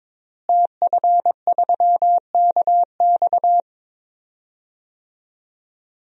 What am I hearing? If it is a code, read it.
Morse code, heard as a keyed tone, about 22 wpm, 700 Hz. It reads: TF3KX
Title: DXCC-100 Call Signs - 5 Characters - Repeated 2x - 22wpm